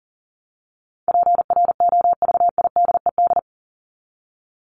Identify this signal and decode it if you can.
Morse code, heard as a keyed tone, about 40 wpm, 700 Hz. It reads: PROVIDED
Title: Top 1751-2000 Words 40wpm